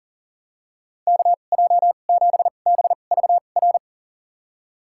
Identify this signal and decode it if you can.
Morse code, heard as a keyed tone, about 40 wpm, spaced as for 29 wpm, 700 Hz. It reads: KJ7BVR
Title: USA Call Signs - 6 Characters - 2x Character Spacing - 40wpm